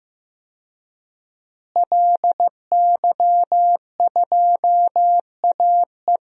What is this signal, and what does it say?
Morse code, heard as a keyed tone, about 15 wpm, 700 Hz. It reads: LY2AE
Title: DXCC-100 Call Signs - 5 Characters - Repeated 2x - 15wpm